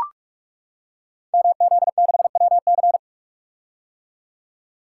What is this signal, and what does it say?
Morse code, heard as a keyed tone, about 44 wpm, 700 Hz. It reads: MZ6WC
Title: Code Groups - 5 Letter:Numbers - 45wpm